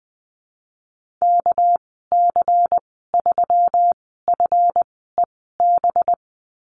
Morse code, heard as a keyed tone, about 20 wpm, spaced as for 15 wpm, 700 Hz. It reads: KC3FEB
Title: 200 Lesson 32g - Letter k - Call Signs 20wpm